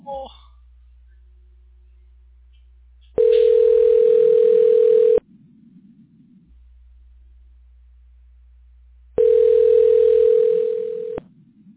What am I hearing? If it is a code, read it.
Ringback tone (North America), 440 + 480 Hz.